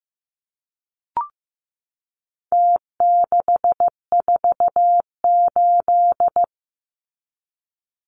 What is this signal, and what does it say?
Morse code, heard as a keyed tone, about 15 wpm, 700 Hz. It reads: T648
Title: Numbers - Contest Style - 4-Digit 15wpm